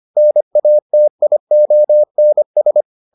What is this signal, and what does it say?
Morse code, heard as a keyed tone, about 25 wpm, 600 Hz. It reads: NATIONS